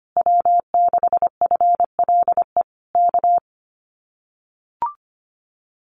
Morse code, heard as a keyed tone, about 25 wpm, 700 Hz. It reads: W6FLE K
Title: Common QSO Phrases Encore 25wpm